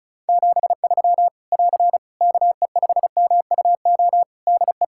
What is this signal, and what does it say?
Morse code, heard as a keyed tone, about 36 wpm, 700 Hz. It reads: 73 <AR> KE5MUO DE
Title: Practice QSO - Exchange 4 - Ending - 35wpm